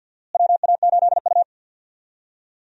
Morse code, heard as a keyed tone, about 46 wpm, 700 Hz. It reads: WA8U
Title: USA Call Signs - 4 Characters - Repeated 2x - 50wpm